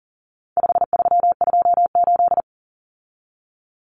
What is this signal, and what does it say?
Morse code, heard as a keyed tone, about 40 wpm, 700 Hz. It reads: 5328
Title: Numbers - 4-Digit 40wpm